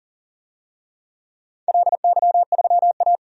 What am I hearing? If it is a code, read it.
Morse code, heard as a keyed tone, about 38 wpm, 700 Hz. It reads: LY3A
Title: DXCC-100 Call Signs - 4 Characters - 40wpm